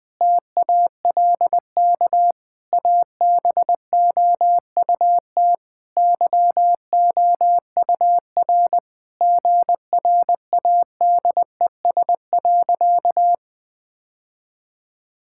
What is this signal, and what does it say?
Morse code, heard as a keyed tone, about 20 wpm, 700 Hz. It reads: TALK ABOUT YOUR GRADES.
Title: Sentences with 3-Word Phrases - 20wpm